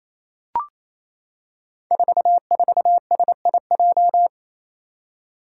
Morse code, heard as a keyed tone, about 27 wpm, 700 Hz. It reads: 44SIJ
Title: Five Challenging Letter-Number Combinations 28wpm